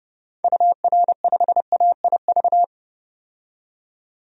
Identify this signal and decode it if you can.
Morse code, heard as a keyed tone, about 30 wpm, 700 Hz. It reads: UR5AIV
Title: DXCC-100 Call Signs - 6 Characters - 30wpm